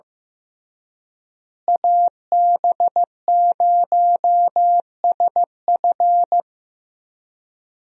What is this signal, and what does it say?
Morse code, heard as a keyed tone, about 16 wpm, 700 Hz. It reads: AB0SF
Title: USA Call Signs - 5 Characters - Repeated 3x - 15wpm